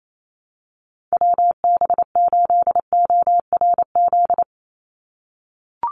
Morse code, heard as a keyed tone, about 28 wpm, 700 Hz. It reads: WB8ORZ
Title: USA Call Signs 28wpm